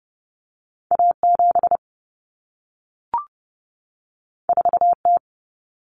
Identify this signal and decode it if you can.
Morse code, heard as a keyed tone, about 30 wpm, 700 Hz. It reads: A7 4T
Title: Common Cut Numbers - 2-Digit 30wpm